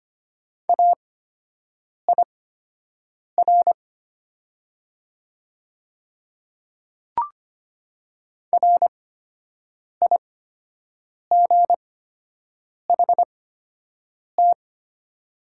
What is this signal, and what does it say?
Morse code, heard as a keyed tone, about 25 wpm, spaced as for 7 wpm, 700 Hz. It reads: AIR RIGHT
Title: 8x Character Spacing - Top 200 Words - 25wpm